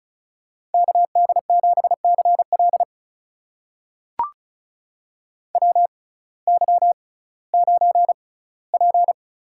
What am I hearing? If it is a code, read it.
Morse code, heard as a keyed tone, about 36 wpm, spaced as for 19 wpm, 700 Hz. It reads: KD7CL WY9P
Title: USA Call Signs - 6x Character Spacing - 35wpm